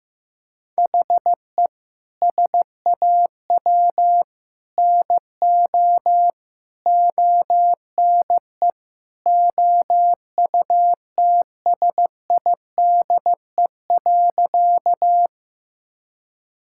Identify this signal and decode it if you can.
Morse code, heard as a keyed tone, about 15 wpm, 700 Hz. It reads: HE SAW NO ONE OUTSIDE.
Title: Sentences from Top 1000 Words - Review Part 6 - 15wpm